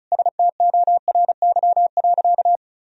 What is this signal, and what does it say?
Morse code, heard as a keyed tone, about 36 wpm, 700 Hz. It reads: STORY.
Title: Sentences from Top 1000 Words - Review Part 1 - 35wpm